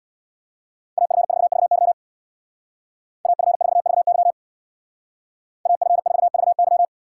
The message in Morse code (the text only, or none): AF4VY AF4VY AF4VY